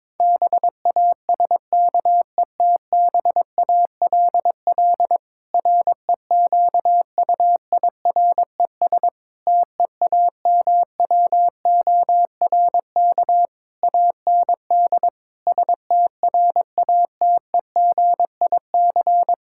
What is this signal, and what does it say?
Morse code, heard as a keyed tone, about 22 wpm, 700 Hz. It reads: BASKETBALL REQUIRES TEAMWORK AND STRATEGIC
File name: Difficult Sentences with Hints - 22wpm